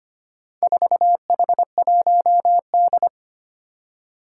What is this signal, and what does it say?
Morse code, heard as a keyed tone, about 25 wpm, 700 Hz. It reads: 4H1D
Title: Four Challenging Letter-Number Combinations 25wpm